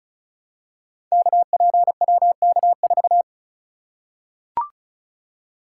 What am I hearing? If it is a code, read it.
Morse code, heard as a keyed tone, about 36 wpm, 700 Hz. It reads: KPWK4
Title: Code Groups - 5 Letter:Numbers - 35wpm